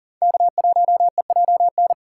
Morse code, heard as a keyed tone, about 40 wpm, 700 Hz. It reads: K1EJN
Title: USA Call Signs - 5 Characters - 4x Character Spacing - 40wpm